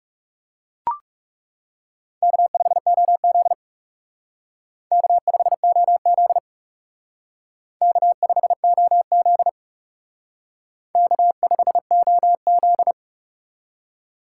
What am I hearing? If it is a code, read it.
Morse code, heard as a keyed tone, about 37 wpm, 700 Hz. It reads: K5OZ K5OZ K5OZ K5OZ
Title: Speed Racing - USA 1x2 Call Signs 30wpm